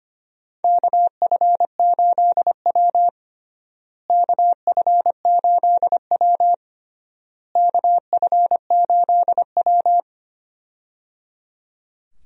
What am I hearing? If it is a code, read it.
Morse code, heard as a keyed tone, about 25 wpm, 700 Hz. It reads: KF8W KF8W KF8W
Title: USA Call Signs - 4 Characters - Repeated 3x - 25wpm